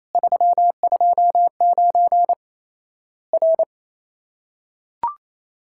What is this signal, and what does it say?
Morse code, heard as a keyed tone, about 28 wpm, 700 Hz. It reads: 329 R
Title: Worked All Europe DX Contest - QTC Exchange - 28wpm